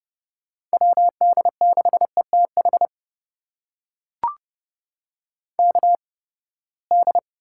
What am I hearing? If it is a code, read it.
Morse code, heard as a keyed tone, about 30 wpm, spaced as for 21 wpm, 700 Hz. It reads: WD6ETH KD
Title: USA Call Signs - 6 Characters - 8x Character Spacing - 30wpm